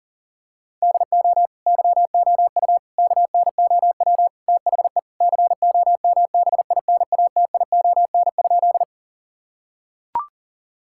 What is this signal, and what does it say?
Morse code, heard as a keyed tone, about 38 wpm, 700 Hz. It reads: DO YOU KNOW THE COMBINATION?
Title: Sentences from Top 1000 Words - Review Part 6 - 40wpm